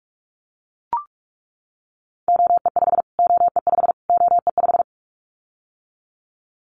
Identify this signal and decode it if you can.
Morse code, heard as a keyed tone, about 44 wpm, 700 Hz. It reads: OE5 OE5 OE5
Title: Callsign Prefixes for Top 25 DXCC - Repeated 3x - 45wpm